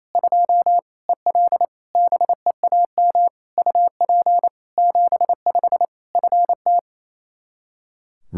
Morse code, heard as a keyed tone, about 28 wpm, 700 Hz. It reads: 2 EL BEAM UP 75 FT
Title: K1USN SST QSO Practice 28wpm